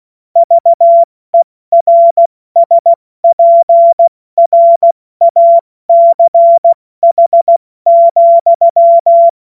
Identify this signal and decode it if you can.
Morse code, heard as a keyed tone, about 16 wpm, 680 Hz. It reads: VERSPRACH,
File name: Morsekurs / Witze / Witz 25. Oktober 2022 70 F80